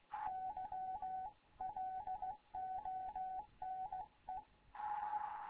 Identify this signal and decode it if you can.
Morse code, heard as a keyed tone, about 16 wpm, 750 Hz, with radio noise. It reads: YLONE
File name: Morze-HA2MN / hangp / 2 11 GYAKORLAT